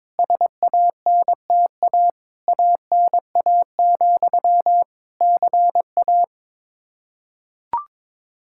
Morse code, heard as a keyed tone, about 22 wpm, 700 Hz. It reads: SANTA ANA, CA
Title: Practice QSO - Exchange 1 - Beginning - 22wpm